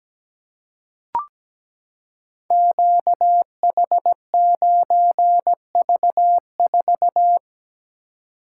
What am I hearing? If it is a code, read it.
Morse code, heard as a keyed tone, about 17 wpm, 700 Hz. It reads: QH9V4
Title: Code Groups - 5 Letter:Numbers - 17wpm